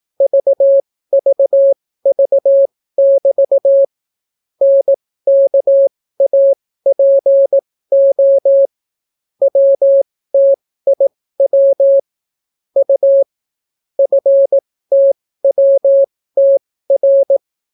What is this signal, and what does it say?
Morse code, heard as a keyed tone, about 18 wpm, spaced as for 14 wpm, 550 Hz. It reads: VVV<BT> NKAPO WTIW U FTWTR